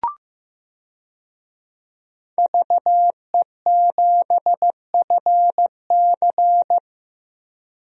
Morse code, heard as a keyed tone, about 15 wpm, 700 Hz. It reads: VE7FC
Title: DXCC-100 Call Signs - 5 Characters - Repeated 2x - 15wpm